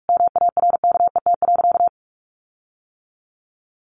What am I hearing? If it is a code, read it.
Morse code, heard as a keyed tone, about 44 wpm, 700 Hz. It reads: MARKET.
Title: Sentences from Top 1400 Words 45wpm